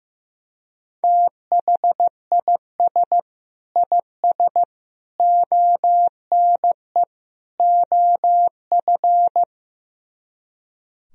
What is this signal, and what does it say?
Morse code, heard as a keyed tone, about 15 wpm, 700 Hz. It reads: THIS IS ONE OF
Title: Speed-Racing - Sets of 4 Words 15wpm